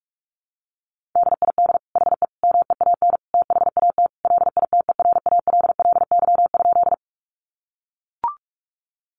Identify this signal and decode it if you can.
Morse code, heard as a keyed tone, about 44 wpm, 700 Hz. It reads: DID HE MEAN THAT LITERALLY?
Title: Sentences from Top 1400 Words 45wpm